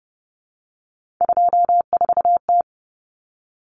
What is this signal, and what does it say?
Morse code, heard as a keyed tone, about 29 wpm, 700 Hz. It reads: 24T
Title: Common Cut Numbers - 3-Digit 30wpm